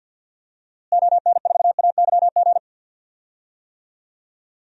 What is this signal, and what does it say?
Morse code, heard as a keyed tone, about 52 wpm, 700 Hz. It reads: ON4AYG
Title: DXCC-100 Call Signs - 6 Characters - Repeated 2x - 50wpm